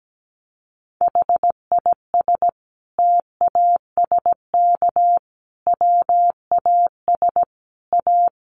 Morse code, heard as a keyed tone, about 17 wpm, 700 Hz. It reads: HIS TASK WAS A
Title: Sentences from Top 1200 Words 17wpm